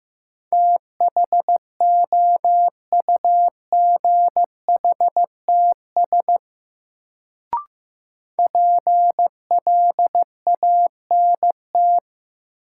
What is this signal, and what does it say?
Morse code, heard as a keyed tone, about 15 wpm, 700 Hz. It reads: THOUGHTS PLANT
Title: Sentences from Top 1000 Words - Review Part 1 - New Word Drill - 15wpm